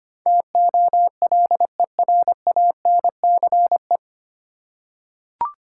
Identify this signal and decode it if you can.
Morse code, heard as a keyed tone, about 25 wpm, 700 Hz. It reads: TOLERANCE
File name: Nine Letter Words - 25wpm